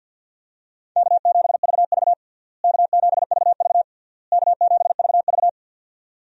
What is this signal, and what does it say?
Morse code, heard as a keyed tone, about 52 wpm, 700 Hz. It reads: K7VV K7VV K7VV
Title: USA Call Signs - 4 Characters - Repeated 3x - 50wpm